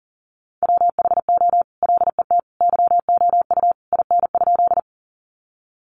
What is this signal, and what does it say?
Morse code, heard as a keyed tone, about 40 wpm, 700 Hz. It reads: WHO LET YOU IN?
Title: Sentences from Top 300 Words 40wpm